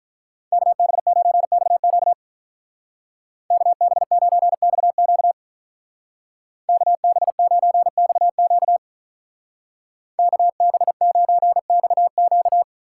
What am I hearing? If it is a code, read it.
Morse code, heard as a keyed tone, about 44 wpm, 700 Hz. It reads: KB9XQ KB9XQ KB9XQ KB9XQ
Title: Speed Racing - USA 2x2 Call Signs 35wpm